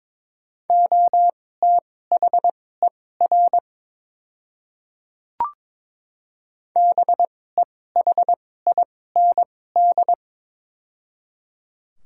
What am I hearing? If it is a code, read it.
Morse code, heard as a keyed tone, about 22 wpm, spaced as for 16 wpm, 700 Hz. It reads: OTHER BEHIND
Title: 2x Character Spacing - Top 500 Words - 22wpm